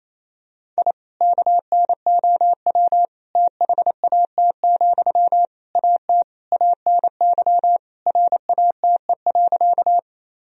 Morse code, heard as a keyed tone, about 28 wpm, 700 Hz. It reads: I KNOW THAT, AT ANY RATE.
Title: Sentences from Top 1000 Words - Review Part 4 - 28wpm